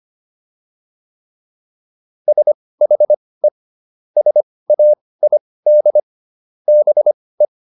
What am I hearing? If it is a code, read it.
Morse code, heard as a keyed tone, about 25 wpm, spaced as for 18 wpm, 600 Hz. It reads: SHE SAID BE